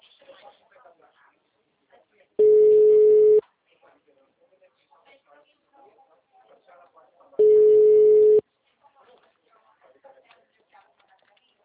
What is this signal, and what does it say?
Ringback tone (Europe, 425 Hz), with speech.